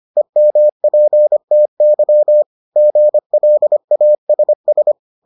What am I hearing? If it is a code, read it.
Morse code, heard as a keyed tone, about 25 wpm, 600 Hz. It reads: EMPTY GLASS